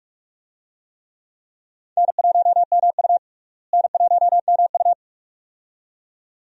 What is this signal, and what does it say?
Morse code, heard as a keyed tone, about 44 wpm, 700 Hz. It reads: N1MU N1MU
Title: USA Call Signs - 4 Characters - Repeated 2x - 45wpm